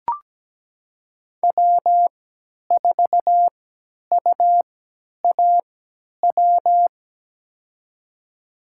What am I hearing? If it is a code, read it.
Morse code, heard as a keyed tone, about 17 wpm, spaced as for 10 wpm, 700 Hz. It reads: W4UAW
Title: USA Call Signs - 5 Characters - 3x Character Spacing - 17wpm